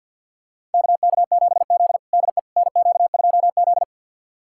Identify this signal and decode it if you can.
Morse code, heard as a keyed tone, about 52 wpm, 690 Hz. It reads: KK7Z DE NQ2Z
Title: USA Callsign DE Callsign - 4 Characters - 50wpm